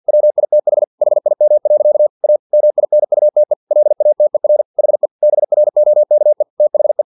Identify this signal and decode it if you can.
Morse code, heard as a keyed tone, about 50 wpm, 600 Hz. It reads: WITH HIM. A MINUTE LATER HE BROKE THE
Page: Never 49wpm